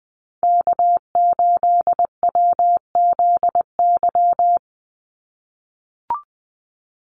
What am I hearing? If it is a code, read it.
Morse code, heard as a keyed tone, about 20 wpm, 700 Hz. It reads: K8WZY
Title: USA Call Signs 20wpm